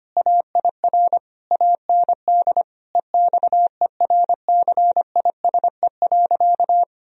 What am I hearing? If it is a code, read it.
Morse code, heard as a keyed tone, about 25 wpm, 700 Hz. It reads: AIR AND EXERCISE.
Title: Sentences from Top 1000 Words - Review Part 4 - 25wpm